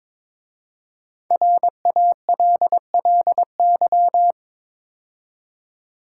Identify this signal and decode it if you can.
Morse code, heard as a keyed tone, about 22 wpm, 700 Hz. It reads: RALLY
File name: 5 Letter Words - First 3 Letters Given - 22wpm